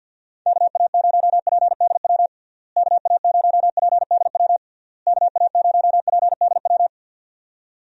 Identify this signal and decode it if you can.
Morse code, heard as a keyed tone, about 51 wpm, 700 Hz. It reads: KA0PDW KA0PDW KA0PDW
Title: USA Call Signs - 6 Characters - Repeated 3x - 50wpm